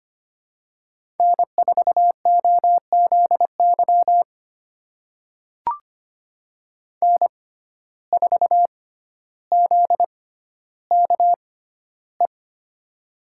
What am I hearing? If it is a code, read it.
Morse code, heard as a keyed tone, about 25 wpm, spaced as for 13 wpm, 700 Hz. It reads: N4OZY N4ZKE
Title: USA Call Signs - 5 Characters - 6x Character Spacing - 25wpm